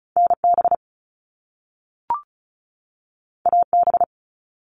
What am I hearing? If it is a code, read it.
Morse code, heard as a keyed tone, about 36 wpm, 700 Hz. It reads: NB AB
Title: Canadian Provinces and Territories 35wpm